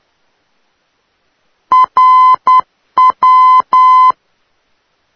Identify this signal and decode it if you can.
Morse code, heard as a keyed tone, about 10 wpm, 1020 Hz. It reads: RW
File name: europe / d-4041-rw-392-tegel-k3-2014-05-rb